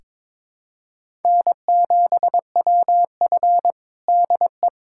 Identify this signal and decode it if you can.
Morse code, heard as a keyed tone, about 22 wpm, 700 Hz. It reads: N7WF DE
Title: USA Callsign DE Callsign - 4 Characters - 22wpm